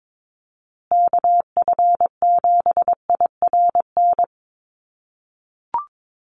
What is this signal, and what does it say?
Morse code, heard as a keyed tone, about 22 wpm, 700 Hz. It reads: KF7IRN
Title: USA Call Signs Repeated 2x 22wpm